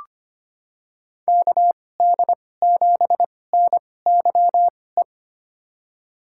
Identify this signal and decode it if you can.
Morse code, heard as a keyed tone, about 25 wpm, spaced as for 18 wpm, 700 Hz. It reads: KD7NYE
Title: USA Call Signs - 6 Characters - 2x Character Spacing - 25wpm